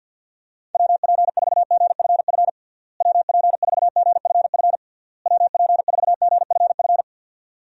Morse code, heard as a keyed tone, about 49 wpm, 700 Hz. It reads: WP4GFF WP4GFF WP4GFF
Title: USA Call Signs - 6 Characters - Repeated 3x - 50wpm